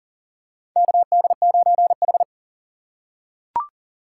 Morse code, heard as a keyed tone, about 40 wpm, 700 Hz. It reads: KD9H
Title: USA Call Signs - 4 Characters - 3x Character Spacing - 40wpm